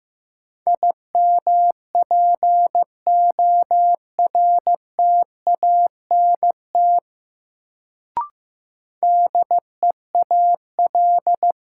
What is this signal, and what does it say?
Morse code, heard as a keyed tone, about 15 wpm, 700 Hz. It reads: IMPORTANT DEAL
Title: Sets of 3 Words - New Word Drill - 15wpm